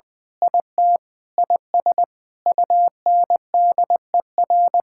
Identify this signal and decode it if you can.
Morse code, heard as a keyed tone, about 21 wpm, 700 Hz. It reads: IT IS UNDER
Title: Sentences with 9-Word Phrases - 20wpm